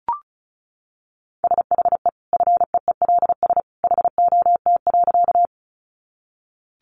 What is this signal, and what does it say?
Morse code, heard as a keyed tone, about 34 wpm, 700 Hz. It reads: SHE FEELS HOT.
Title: Easy Phrases 35wpm